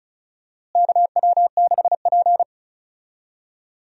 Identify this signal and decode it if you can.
Morse code, heard as a keyed tone, about 36 wpm, 700 Hz. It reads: KW6P
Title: USA Call Signs - 4 Characters - Repeated 2x - 35wpm